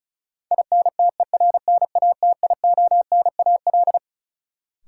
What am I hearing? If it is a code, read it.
Morse code, heard as a keyed tone, about 36 wpm, 700 Hz. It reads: INTERNATIONAL
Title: Sentences from Top 1000 Words - Review Part 4 - New Word Drill - 35wpm